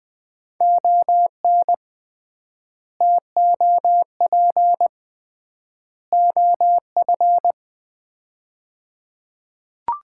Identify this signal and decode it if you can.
Morse code, heard as a keyed tone, about 21 wpm, 700 Hz. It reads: ON TOP OF
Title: 3x Word-Spacing - Sets of 3 Words - 20wpm